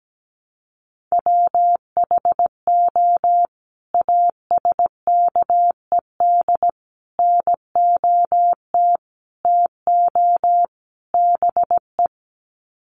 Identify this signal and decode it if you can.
Morse code, heard as a keyed tone, about 17 wpm, 700 Hz. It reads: WHO ASKED NOT TO BE
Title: Sets of 5 Words Encore2 17wpm